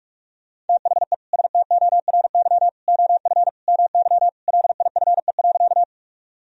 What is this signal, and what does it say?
Morse code, heard as a keyed tone, about 46 wpm, 700 Hz. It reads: THE STORY OF MY LIFE.
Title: Sentences from Top 1000 Words - Review Part 1 - 45wpm